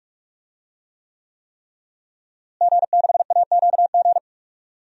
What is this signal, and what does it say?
Morse code, heard as a keyed tone, about 46 wpm, 700 Hz. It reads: G6AQG
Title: DXCC-100 Call Signs - 5 Characters - Repeated 3x - 45wpm